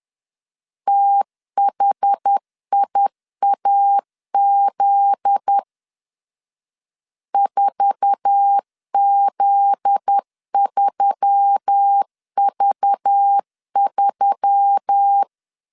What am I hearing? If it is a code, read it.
Morse code, heard as a keyed tone, about 10 wpm, 800 Hz. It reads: THIAZ 4Z3V3
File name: LEZIONI / 9 2 10wpm 2736